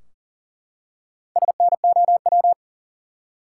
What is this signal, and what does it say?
Morse code, heard as a keyed tone, about 40 wpm, 700 Hz. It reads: SNOW
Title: Four Letter Words - 40wpm